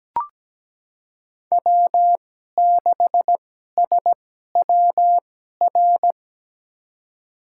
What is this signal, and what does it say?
Morse code, heard as a keyed tone, about 17 wpm, spaced as for 12 wpm, 700 Hz. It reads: W6SWR
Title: USA Call Signs - 5 Characters - 2x Character Spacing - 17wpm